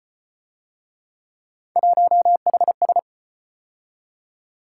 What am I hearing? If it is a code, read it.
Morse code, heard as a keyed tone, about 34 wpm, 700 Hz. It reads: 1HS